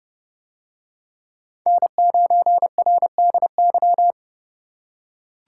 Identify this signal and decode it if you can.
Morse code, heard as a keyed tone, about 31 wpm, 700 Hz. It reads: N9RDY